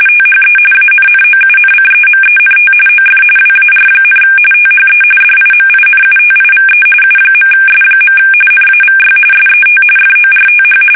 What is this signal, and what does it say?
RTTY, about 45.45 baud. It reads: THE ANNOUNCEMENT OF WAR, THE DIMINUTIVE U.S. NAVY SET FORTH
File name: fsk850 / usnavy2